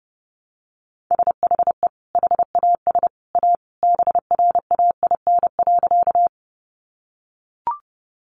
Morse code, heard as a keyed tone, about 30 wpm, 700 Hz. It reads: SHE HAS A BRAIN.
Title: Sentences from Top 1100 Words 30wpm